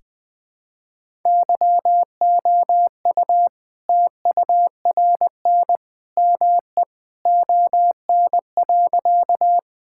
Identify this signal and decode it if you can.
Morse code, heard as a keyed tone, about 21 wpm, 700 Hz. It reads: YOU TURN ME ON.
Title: All-Age Appropriate Jokes - 20wpm